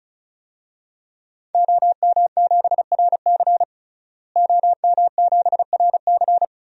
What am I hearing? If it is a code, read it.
Morse code, heard as a keyed tone, about 36 wpm, 700 Hz. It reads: OM7RC OM7RC
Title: DXCC-100 Call Signs - 5 Characters - Repeated 2x - 35wpm